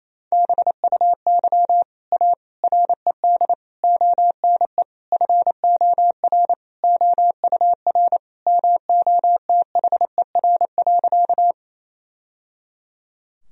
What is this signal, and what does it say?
Morse code, heard as a keyed tone, about 28 wpm, 700 Hz. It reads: BUY A RED ONE FOR OUR MOTHER.
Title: Sentences from Top 1000 Words - Review Part 2 - 28wpm